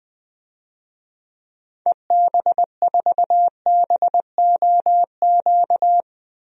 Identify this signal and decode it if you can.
Morse code, heard as a keyed tone, about 20 wpm, 700 Hz. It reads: EB4BOQ
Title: DXCC-100 Call Signs - 6 Characters - Repeated 3x - 20wpm